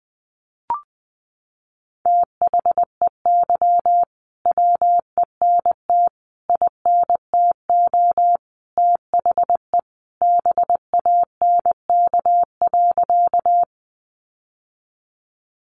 Morse code, heard as a keyed tone, about 20 wpm, 700 Hz. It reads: THEY WENT INTO THE BANK.
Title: Sentences from Top 900 Words 20wpm